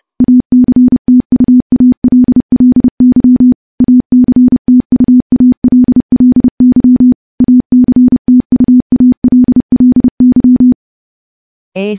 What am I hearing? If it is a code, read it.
Morse code, heard as a keyed tone, about 30 wpm, 250 Hz. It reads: ACTUALLY ACTUALLY ACTUALLY
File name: morse-training-30WPM